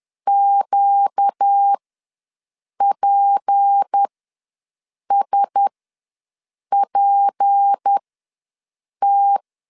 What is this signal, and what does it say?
Morse code, heard as a keyed tone, about 10 wpm, spaced as for 6 wpm, 800 Hz. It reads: QPSPT
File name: LEZIONI / 6 PKX0 / 6 P ws 10wpm 3751